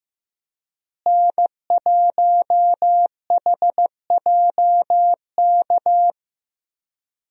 Morse code, heard as a keyed tone, about 15 wpm, 700 Hz. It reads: N1HJK
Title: USA Call Signs - 5 Characters - Repeated 2x - 15wpm